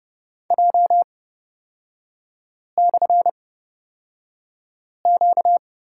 Morse code, heard as a keyed tone, about 30 wpm, spaced as for 5 wpm, 700 Hz. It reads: J/Q
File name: Lesson 38e - Letter q - Three Characters 30wpm